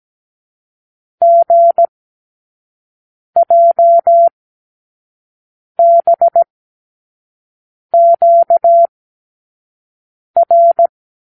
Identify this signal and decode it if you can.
Morse code, heard as a keyed tone, about 17 wpm, spaced as for 5 wpm, 680 Hz. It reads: GJBQR
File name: k7qo / Lesson 36 (A-X)